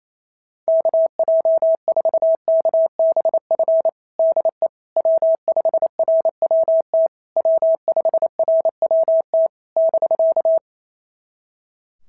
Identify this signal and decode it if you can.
Morse code, heard as a keyed tone, about 27 wpm, 650 Hz. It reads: KJ4KBF DE W5RWT W5RWT <BK>
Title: SOTA QSO Practice 28wpm